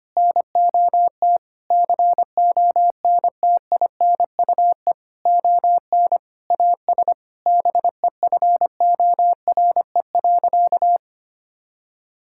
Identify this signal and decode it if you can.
Morse code, heard as a keyed tone, about 25 wpm, 700 Hz. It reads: NOT CONTINUE ON AS BEFORE.
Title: Sentences from Top 1000 Words - Review Part 6 - 25wpm